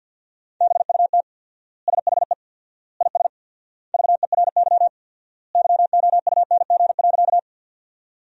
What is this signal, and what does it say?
Morse code, heard as a keyed tone, about 50 wpm, 700 Hz. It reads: BUT SHE IS VERY YOUNG.
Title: 4x Word-Spacing - Sentences from Top 400 Words - 50wpm